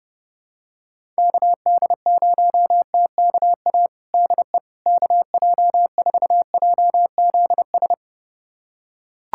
Morse code, heard as a keyed tone, about 30 wpm, 700 Hz. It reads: KD0TKA DE KJ4JZS
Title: USA Callsign DE Callsign - 6 Characters - 30wpm